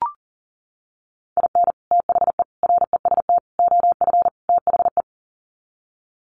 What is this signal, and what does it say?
Morse code, heard as a keyed tone, about 40 wpm, 700 Hz. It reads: IN THE REST OF THE
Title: Sets of 5 Words 40wpm